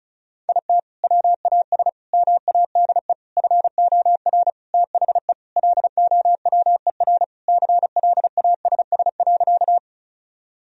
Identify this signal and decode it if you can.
Morse code, heard as a keyed tone, about 34 wpm, 700 Hz. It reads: IT WAS MADE FOR THE LOWER CLASS.
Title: Sentences from Top 1000 Words - Review Part 4 - 35wpm